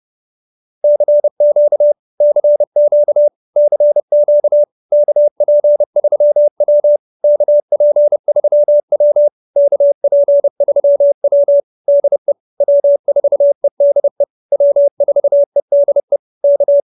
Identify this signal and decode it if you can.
Morse code, heard as a keyed tone, about 30 wpm, 590 Hz. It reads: CQ CQ CQ KP3W KP3W KP3W DE W4EDE W4EDE K